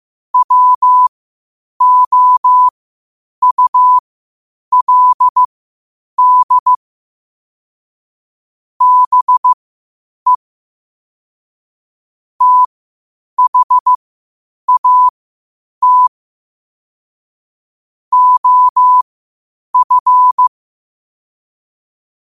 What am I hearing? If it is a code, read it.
Morse code, heard as a keyed tone, about 15 wpm, spaced as for 8 wpm, 1000 Hz. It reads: WOULD BE THAT OF
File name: qotdfiles / 8wpm